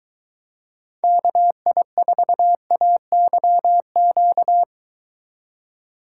Morse code, heard as a keyed tone, about 22 wpm, 700 Hz. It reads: KI4AYQ